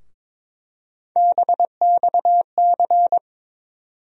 Morse code, heard as a keyed tone, about 22 wpm, 700 Hz. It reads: BXC
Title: USA Call Sign Postfixes - 3 Letters 22wpm